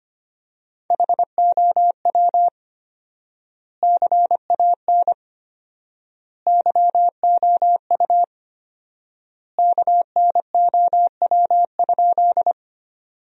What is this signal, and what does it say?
Morse code, heard as a keyed tone, about 25 wpm, 700 Hz. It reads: HOW CAN YOU KNOW?
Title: 4x Word-Spacing - Sentences from Top 100 Words - 25wpm